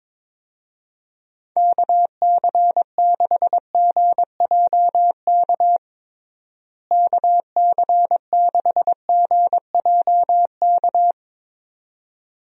Morse code, heard as a keyed tone, about 22 wpm, 700 Hz. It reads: KC6GJK KC6GJK
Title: USA Call Signs - 6 Characters - Repeated 2x - 22wpm